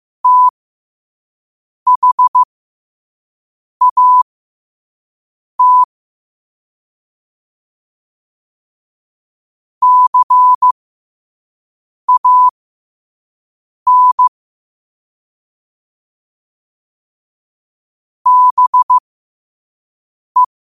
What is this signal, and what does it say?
Morse code, heard as a keyed tone, about 15 wpm, spaced as for 5 wpm, 1000 Hz. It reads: THAT CAN BE
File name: qotdfiles / 5wpm